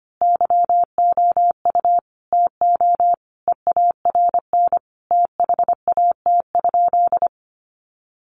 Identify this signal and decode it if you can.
Morse code, heard as a keyed tone, about 25 wpm, 700 Hz. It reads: YOU TO EARN THAT?
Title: Sentences from Top 1500 Words 25wpm